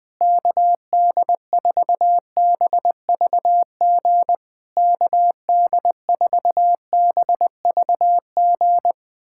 Morse code, heard as a keyed tone, about 20 wpm, 700 Hz. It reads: KD4BVG KD4BVG
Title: POTA QSO Practice - 20wpm